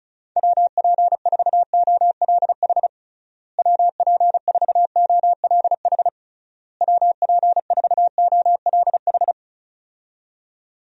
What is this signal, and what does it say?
Morse code, heard as a keyed tone, about 34 wpm, 700 Hz. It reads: WP4OLH WP4OLH WP4OLH
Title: USA Call Signs - 6 Characters - Repeated 3x - 35wpm